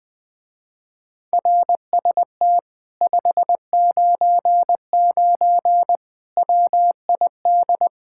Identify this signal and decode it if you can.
Morse code, heard as a keyed tone, about 19 wpm, 700 Hz. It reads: RST 599 WID
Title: Speed-Racing Common QSO Phrases - 20wpm